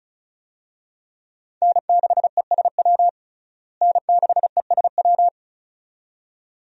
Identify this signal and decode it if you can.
Morse code, heard as a keyed tone, about 34 wpm, 700 Hz. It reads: N6ESW N6ESW
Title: USA Call Signs - 5 Characters - Repeated 2x - 35wpm